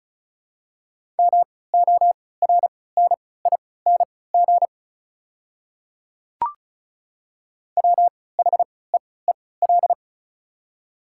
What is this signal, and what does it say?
Morse code, heard as a keyed tone, about 36 wpm, spaced as for 20 wpm, 690 Hz. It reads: MORNING WHEEL
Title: 3x Character Spacing - Top 500 Words - 35wpm